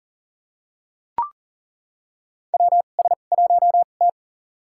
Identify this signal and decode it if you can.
Morse code, heard as a keyed tone, about 40 wpm, spaced as for 29 wpm, 700 Hz. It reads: WS1T
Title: USA Call Signs - 2x Character Spacing - 40wpm